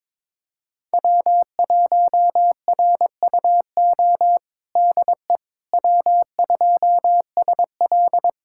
Morse code, heard as a keyed tone, about 22 wpm, 700 Hz. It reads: W1RUO DE W2SL
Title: USA Callsign DE Callsign 22wpm